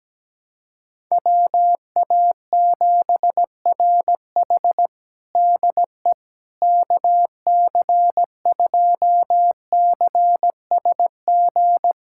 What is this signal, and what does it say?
Morse code, heard as a keyed tone, about 17 wpm, 700 Hz. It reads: WA7RH DE KC2CSG
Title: USA Callsign DE Callsign 17wpm